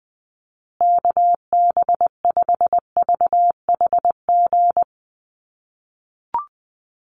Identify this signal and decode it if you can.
Morse code, heard as a keyed tone, about 20 wpm, 700 Hz. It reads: KB5VHG
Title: USA Call Signs Repeated 3x 20wpm